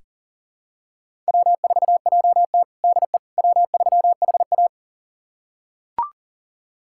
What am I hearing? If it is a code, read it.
Morse code, heard as a keyed tone, about 37 wpm, 700 Hz. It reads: W4JT DE W3HA
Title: USA Callsign DE Callsign - 4 Characters - 40wpm